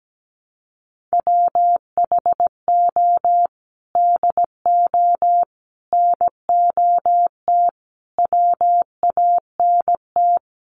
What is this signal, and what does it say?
Morse code, heard as a keyed tone, about 17 wpm, 700 Hz. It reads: WHO DO NOT WANT
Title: Sets of 5 Words Encore2 17wpm